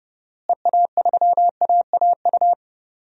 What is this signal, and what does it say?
Morse code, heard as a keyed tone, about 30 wpm, 700 Hz. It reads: EA3AAU
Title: DXCC-100 Call Signs - 6 Characters - Repeated 3x - 30wpm